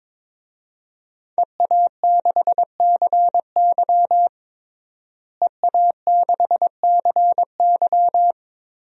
Morse code, heard as a keyed tone, about 22 wpm, 700 Hz. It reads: EA6CY EA6CY
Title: DXCC-100 Call Signs - 5 Characters - Repeated 2x - 22wpm